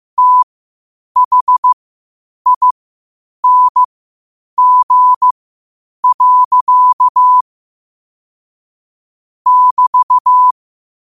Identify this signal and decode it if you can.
Morse code, heard as a keyed tone, about 15 wpm, spaced as for 8 wpm, 1000 Hz. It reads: THING. <BT>
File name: qotdfiles / 8wpm